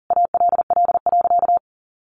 Morse code, heard as a keyed tone, about 40 wpm, 700 Hz. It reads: ALL.
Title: Sentences from Top 1400 Words 40wpm